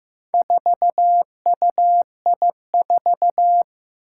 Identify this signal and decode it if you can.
Morse code, heard as a keyed tone, about 15 wpm, 700 Hz. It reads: 4UI4
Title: Four Challenging Letter-Number Combinations 15wpm